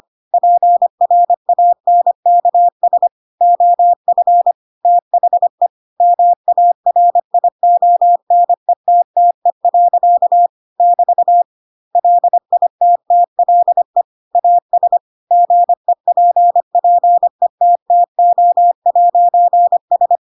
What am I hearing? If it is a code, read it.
Morse code, heard as a keyed tone, about 25 wpm, 700 Hz. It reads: PRANKS OF THE MARIONETTE. <BT> LITTLE AS GEPPETTO'S
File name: Pinocchio / Pinocchio ch3 25 25 ik1hge